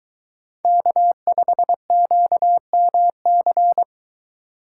Morse code, heard as a keyed tone, about 22 wpm, 700 Hz. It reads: K5QMC